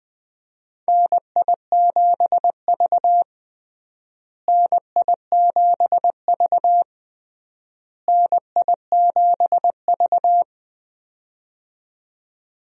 Morse code, heard as a keyed tone, about 20 wpm, 700 Hz. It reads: NI7V NI7V NI7V
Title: USA Call Signs - 4 Characters - Repeated 3x - 20wpm